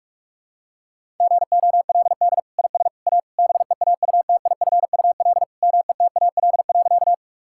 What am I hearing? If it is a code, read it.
Morse code, heard as a keyed tone, about 46 wpm, 700 Hz. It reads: GOLD IS A BEAUTIFUL METAL.
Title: Sentences from Top 1000 Words - Review Part 6 - 45wpm